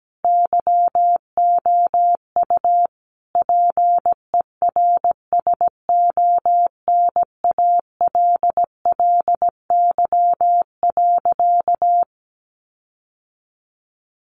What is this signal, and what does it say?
Morse code, heard as a keyed tone, about 17 wpm, 700 Hz. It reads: YOU PERSONALLY.
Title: Sentences from Top 1400 Words 17wpm